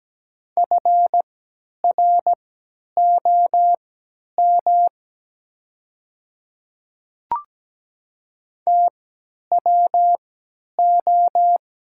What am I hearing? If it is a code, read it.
Morse code, heard as a keyed tone, about 17 wpm, spaced as for 10 wpm, 700 Hz. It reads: FROM TWO
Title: 3x Character Spacing - Top 100 Words - 17wpm